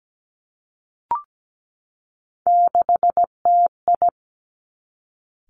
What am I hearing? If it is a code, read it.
Morse code, heard as a keyed tone, about 17 wpm, 700 Hz. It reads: 6TI
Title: Three Challenging Letter-Number Combinations 17wpm